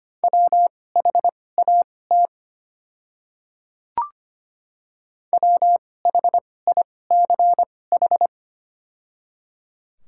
Morse code, heard as a keyed tone, about 25 wpm, spaced as for 18 wpm, 690 Hz. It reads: WHAT WHICH
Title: 2x Character Spacing - Top 100 Words - 25wpm